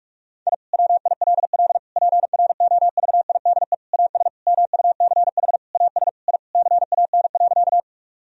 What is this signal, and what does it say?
Morse code, heard as a keyed tone, about 44 wpm, 700 Hz. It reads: I WILL PROVIDE AS MUCH AS I CAN.
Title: Sentences from Top 1000 Words - Review Part 3 - 45wpm